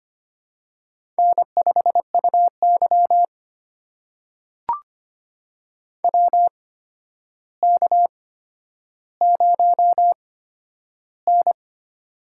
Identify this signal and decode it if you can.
Morse code, heard as a keyed tone, about 25 wpm, spaced as for 11 wpm, 700 Hz. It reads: N5UY WK0N
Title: USA Call Signs - 4 Characters - 8x Character Spacing - 25wpm